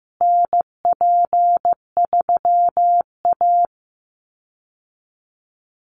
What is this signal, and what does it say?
Morse code, heard as a keyed tone, about 15 wpm, 700 Hz. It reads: NP3A
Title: Call Signs Repeated 3x - 15wpm